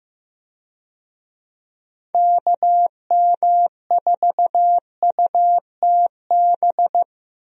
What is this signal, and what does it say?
Morse code, heard as a keyed tone, about 16 wpm, 700 Hz. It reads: KM4UTB
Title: USA Call Signs - 6 Characters - Repeated 2x - 15wpm